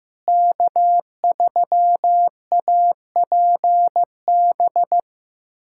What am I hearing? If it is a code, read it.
Morse code, heard as a keyed tone, about 15 wpm, 700 Hz. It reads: K3APB
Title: USA Callsign DE Callsign - 5 Characters - 15wpm